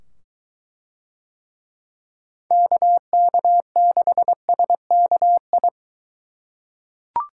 Morse code, heard as a keyed tone, about 21 wpm, 700 Hz. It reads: KK6SKI